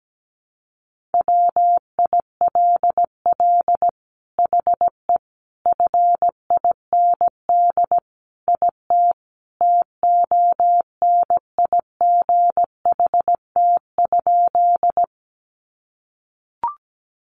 Morse code, heard as a keyed tone, about 17 wpm, 700 Hz. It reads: WILL HE FIND IT TONIGHT?
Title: Sentences from Top 600 Words 17wpm